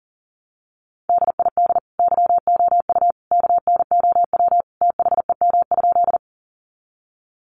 Morse code, heard as a keyed tone, about 40 wpm, 700 Hz. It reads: DID YOU KNOW THEM?
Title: Sentences from Top 100 Words 40wpm